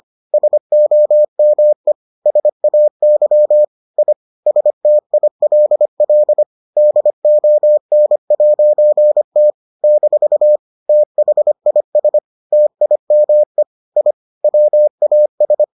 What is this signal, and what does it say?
Morse code, heard as a keyed tone, about 25 wpm, 600 Hz. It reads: SOME SAY I STILL DON'T - THIS TIME I WAS